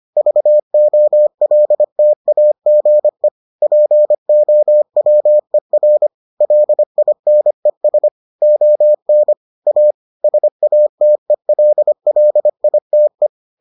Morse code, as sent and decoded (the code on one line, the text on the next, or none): ...- --- .-.. - .- --. / .--. --- .-- . .-. / .-.. .. -. ... / --- -. / .- / ... .- - . .-.. .-.. .. - .
VOLTAGE POWER LINES ON A SATELLITE